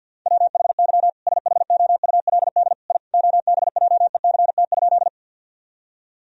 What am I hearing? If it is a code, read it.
Morse code, heard as a keyed tone, about 50 wpm, 700 Hz. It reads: WHY SHOULD I OBJECT?
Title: Sentences from Top 1000 Words - Review Part 3 - 50wpm